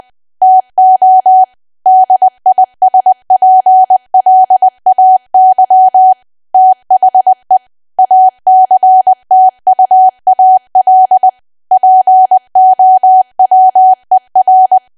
Morse code, heard as a keyed tone, about 20 wpm, 750 Hz. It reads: TO DISPLAY THE ACTUAL POWER
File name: cwa / W1AW-Level3 / 20WPM